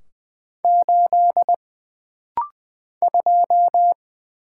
Morse code, heard as a keyed tone, about 20 wpm, 700 Hz. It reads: 8 2
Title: Single Letter-Number - Rapid-Fire 20wpm